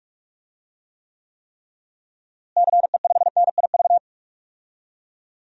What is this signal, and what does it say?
Morse code, heard as a keyed tone, about 41 wpm, 700 Hz. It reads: CE5NIV
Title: DXCC-100 Call Signs - 6 Characters - Repeated 3x - 45wpm